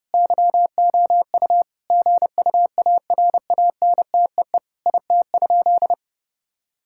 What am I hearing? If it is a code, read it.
Morse code, heard as a keyed tone, about 32 wpm, 700 Hz. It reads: YOU GUARANTEE IT?
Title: Sentences from Top 1600 Words 30wpm